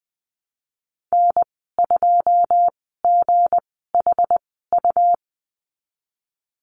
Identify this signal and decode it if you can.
Morse code, heard as a keyed tone, about 20 wpm, spaced as for 15 wpm, 700 Hz. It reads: N2GHU
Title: 167 Lesson 27g - Number 7 - Call Signs 20wpm